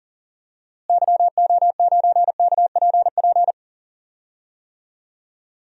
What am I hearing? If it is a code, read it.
Morse code, heard as a keyed tone, about 38 wpm, 700 Hz. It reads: YO9KPP
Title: DXCC-100 Call Signs - 6 Characters - Repeated 2x - 40wpm